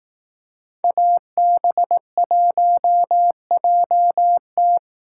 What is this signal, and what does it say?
Morse code, heard as a keyed tone, about 18 wpm, 700 Hz. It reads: AB1JT
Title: Speed Racing - USA 2x2 Call Signs 15wpm